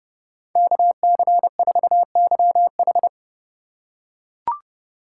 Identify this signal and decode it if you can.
Morse code, heard as a keyed tone, about 26 wpm, 700 Hz. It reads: KC4YH